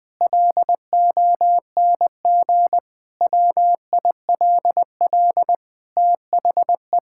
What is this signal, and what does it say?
Morse code, heard as a keyed tone, about 21 wpm, 700 Hz. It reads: LONG WILL THE
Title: Sentences from Top 1000 Words - Review Part 6 - 20wpm